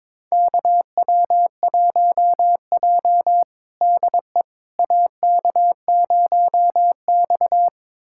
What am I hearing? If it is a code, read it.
Morse code, heard as a keyed tone, about 22 wpm, 700 Hz. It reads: KW1J DE AK0X
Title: USA Callsign DE Callsign - 4 Characters - 22wpm